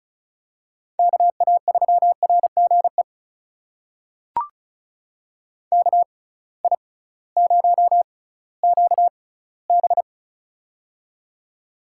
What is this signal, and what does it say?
Morse code, heard as a keyed tone, about 35 wpm, spaced as for 19 wpm, 700 Hz. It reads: KA3RGE KI0QB
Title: USA Call Signs - 6x Character Spacing - 35wpm